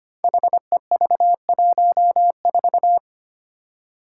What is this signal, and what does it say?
Morse code, heard as a keyed tone, about 25 wpm, 700 Hz. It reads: HEV14
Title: Five Challenging Letter-Number Combinations 25wpm